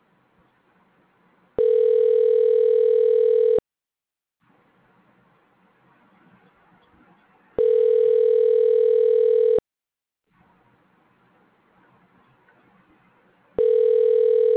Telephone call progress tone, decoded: ringback tone

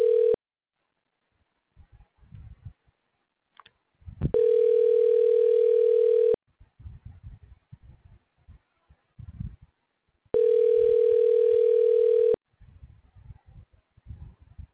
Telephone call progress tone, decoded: ringback tone

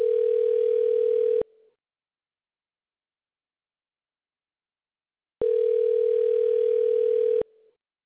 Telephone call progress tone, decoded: ringback tone

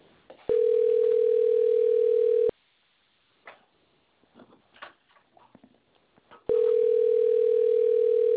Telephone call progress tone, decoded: ringback tone

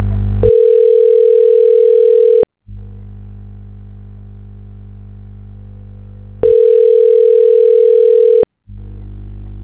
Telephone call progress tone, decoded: ringback tone

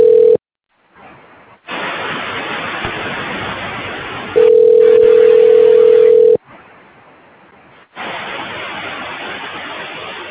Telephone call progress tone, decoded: ringback tone

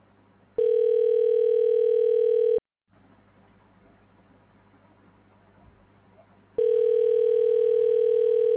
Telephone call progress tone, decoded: ringback tone

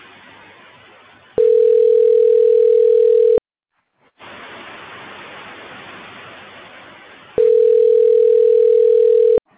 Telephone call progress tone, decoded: ringback tone

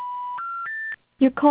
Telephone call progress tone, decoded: special information tone

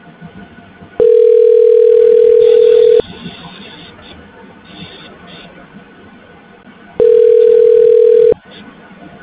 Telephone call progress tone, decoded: ringback tone

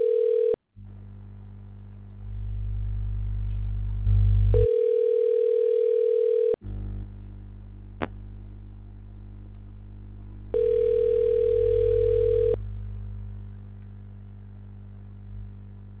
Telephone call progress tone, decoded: ringback tone

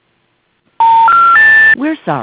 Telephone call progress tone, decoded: special information tone

